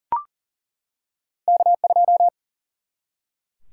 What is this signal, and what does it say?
Morse code, heard as a keyed tone, about 39 wpm, 700 Hz. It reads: K2
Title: Single-Letter Single-Number 40wpm